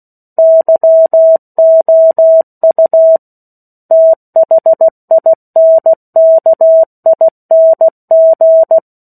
Morse code, heard as a keyed tone, about 16 wpm, 650 Hz. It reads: YOU THINKING
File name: 03-Chapter-03 16 WPM